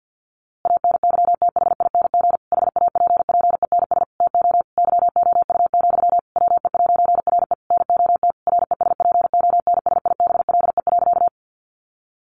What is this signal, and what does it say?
Morse code, heard as a keyed tone, about 50 wpm, 700 Hz. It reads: ANYTHING HAPPENS TO YOU, WE'RE NOT RESPONSIBLE.
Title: A Sound Of Thunder 50wpm